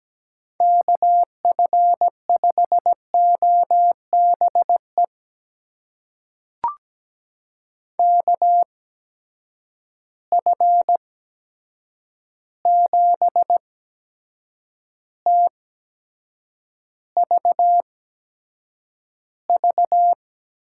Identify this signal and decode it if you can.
Morse code, heard as a keyed tone, about 17 wpm, spaced as for 7 wpm, 700 Hz. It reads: KF5OBE KF7TVV
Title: USA Call Signs - 6 Characters - 8x Character Spacing - 17wpm